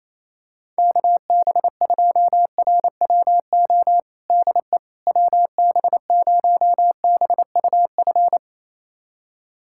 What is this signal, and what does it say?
Morse code, heard as a keyed tone, about 27 wpm, 700 Hz. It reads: KB2RWO DE WB0BUF
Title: USA Callsign DE Callsign - 6 Characters - 28wpm